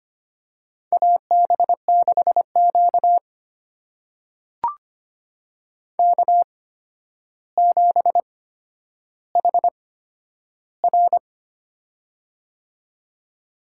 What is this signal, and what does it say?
Morse code, heard as a keyed tone, about 25 wpm, spaced as for 11 wpm, 700 Hz. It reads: AB6Q K7HR
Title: USA Call Signs - 8x Character Spacing - 25wpm